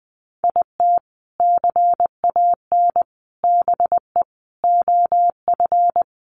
Morse code, heard as a keyed tone, about 20 wpm, 700 Hz. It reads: IT CAN BE OF
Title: Sentences from Top 1200 Words 20wpm